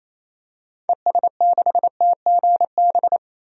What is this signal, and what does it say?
Morse code, heard as a keyed tone, about 28 wpm, 700 Hz. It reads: ES6TGB
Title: DXCC-100 Call Signs - 6 Characters - 28wpm